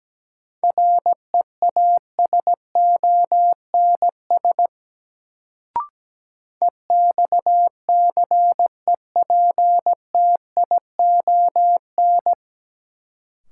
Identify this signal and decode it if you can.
Morse code, heard as a keyed tone, about 17 wpm, 700 Hz. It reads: REASONS EXCEPTION
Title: Sets of 4 Words - New Word Drill - 17wpm